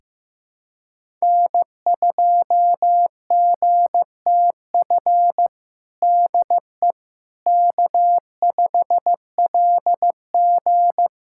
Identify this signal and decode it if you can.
Morse code, heard as a keyed tone, about 15 wpm, 700 Hz. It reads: N2GTF DE K5LG
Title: Practice QSO - Exchange 4 - Ending - 15wpm